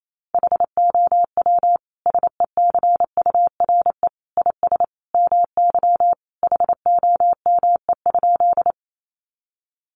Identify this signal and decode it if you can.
Morse code, heard as a keyed tone, about 28 wpm, 700 Hz. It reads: HOW SECURE IS MY HOME?
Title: Sentences from Top 1200 Words 28wpm